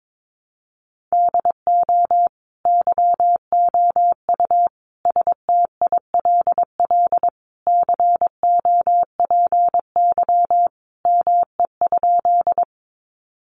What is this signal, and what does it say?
Morse code, heard as a keyed tone, about 22 wpm, 700 Hz. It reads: DO YOU STILL COPY ME?
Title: Sentences from Top 1400 Words 22wpm